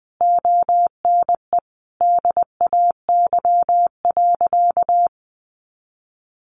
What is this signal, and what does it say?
Morse code, heard as a keyed tone, about 20 wpm, 700 Hz. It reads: ONE DAY.
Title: Sentences from Top 300 Words 20wpm